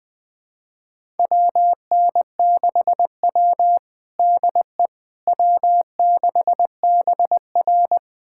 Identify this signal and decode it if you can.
Morse code, heard as a keyed tone, about 20 wpm, 700 Hz. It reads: WN6W DE W6BR
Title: USA Callsign DE Callsign - 4 Characters - 20wpm